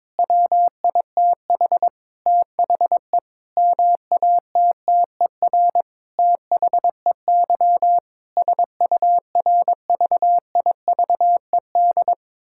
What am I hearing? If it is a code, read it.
Morse code, heard as a keyed tone, about 22 wpm, 700 Hz. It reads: WITH THE MATTER THEY SURVIVED
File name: Sentences with 7-Word Phrases - 22wpm